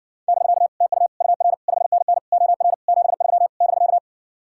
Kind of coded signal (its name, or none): Morse code